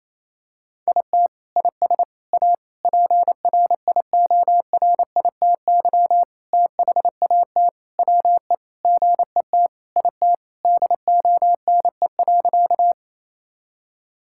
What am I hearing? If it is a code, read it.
Morse code, heard as a keyed tone, about 28 wpm, 700 Hz. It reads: IT IS A PRIORITY THAT WE GET IT DONE.
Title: Sentences from Top 1600 Words 28wpm